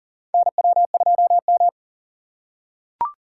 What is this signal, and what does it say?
Morse code, heard as a keyed tone, about 40 wpm, 700 Hz. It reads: NW2M
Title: USA Call Signs - 4x Character Spacing - 40wpm